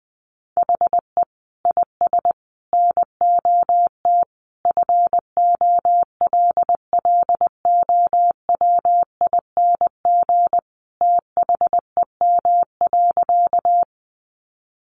Morse code, heard as a keyed tone, about 20 wpm, 700 Hz. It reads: HE IS NOT FOLLOWING THEM.
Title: Sentences from Top 700 Words 20wpm